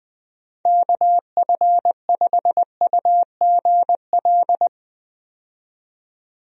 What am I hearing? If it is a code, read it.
Morse code, heard as a keyed tone, about 20 wpm, 700 Hz. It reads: KF5UGL
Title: USA Call Signs - 6 Characters - 20wpm